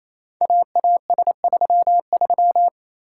Morse code, heard as a keyed tone, about 27 wpm, 700 Hz. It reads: AAS33
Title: Code Groups - 5 Letter:Numbers - 28wpm